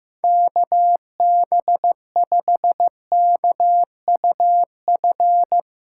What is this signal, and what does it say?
Morse code, heard as a keyed tone, about 16 wpm, 700 Hz. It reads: KB5KUF
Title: Speed Racing - USA 2x3 Call Signs 15wpm